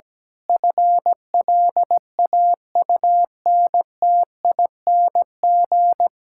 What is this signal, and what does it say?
Morse code, heard as a keyed tone, about 17 wpm, 700 Hz. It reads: FLAUNTING
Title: Pangram Challenge v2 - 17wpm